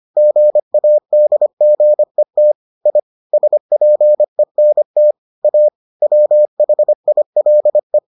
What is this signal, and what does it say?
Morse code, heard as a keyed tone, about 25 wpm, 600 Hz. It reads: GADGET I SPENT A WHILE